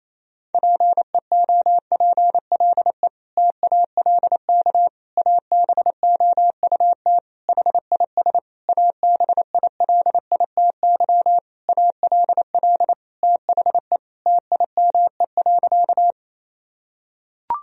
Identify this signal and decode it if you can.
Morse code, heard as a keyed tone, about 28 wpm, 700 Hz. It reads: PEOPLE TALK ABOUT HIS ABILITY ALL THE TIME.
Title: Sentences from Top 1000 Words - Review Part 3 - 28wpm